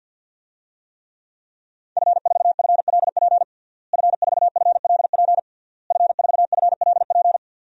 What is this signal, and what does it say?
Morse code, heard as a keyed tone, about 47 wpm, 700 Hz. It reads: F4FLP F4FLP F4FLP
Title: DXCC-100 Call Signs - 5 Characters - Repeated 3x - 50wpm